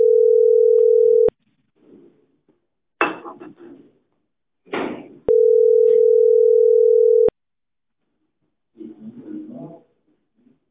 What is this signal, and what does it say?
Ringback tone (North America), 440 + 480 Hz.